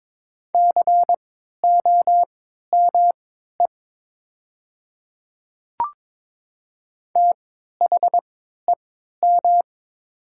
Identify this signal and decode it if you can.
Morse code, heard as a keyed tone, about 22 wpm, spaced as for 12 wpm, 700 Hz. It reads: COME THEM
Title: 3x Character Spacing - Top 100 Words - 22wpm